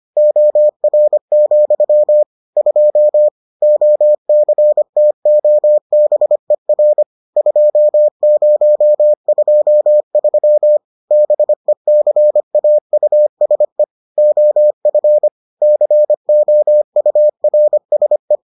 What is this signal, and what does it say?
Morse code, heard as a keyed tone, about 25 wpm, 600 Hz. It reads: OR, 2 OCTOBER 2023 BECAUSE OF COURSE